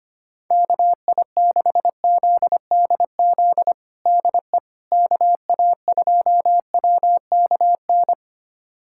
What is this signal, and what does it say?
Morse code, heard as a keyed tone, about 25 wpm, 700 Hz. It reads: KI6ZDZ DE KA2WKN
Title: USA Callsign DE Callsign 25wpm